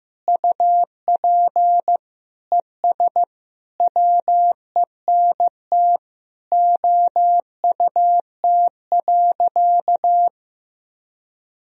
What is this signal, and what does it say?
Morse code, heard as a keyed tone, about 15 wpm, 700 Hz. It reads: UP ES WENT OUT.
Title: Sentences from Top 300 Words and Abbreviations 15wpm